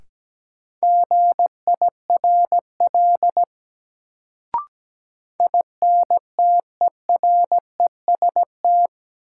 Morse code, heard as a keyed tone, about 17 wpm, 700 Hz. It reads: GIRL INTEREST
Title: Sentences from Top 400 Words - New Word Drill - 17wpm